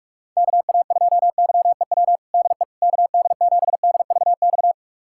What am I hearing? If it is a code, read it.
Morse code, heard as a keyed tone, about 46 wpm, 700 Hz. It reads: KA2YEW DE KD7DVX
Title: USA Callsign DE Callsign - 6 Characters - 45wpm